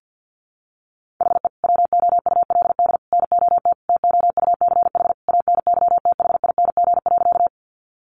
Morse code, heard as a keyed tone, about 50 wpm, 700 Hz. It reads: HE WOULD NOT TOUCH ANYTHING.
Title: Sentences from Top 1200 Words 50wpm